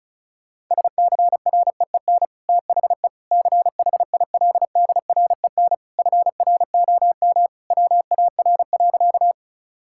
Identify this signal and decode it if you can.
Morse code, heard as a keyed tone, about 36 wpm, 700 Hz. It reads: SCREEN THE CHILDREN FROM WAR.
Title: Sentences from Top 1000 Words - Review Part 6 - 35wpm